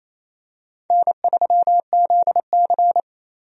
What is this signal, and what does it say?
Morse code, heard as a keyed tone, about 28 wpm, 700 Hz. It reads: N3ZC
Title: USA Call Signs - 4 Characters - 28wpm